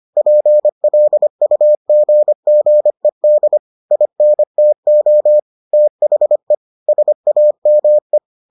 Morse code, heard as a keyed tone, about 25 wpm, 600 Hz. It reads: PLUGGED INTO THE SAME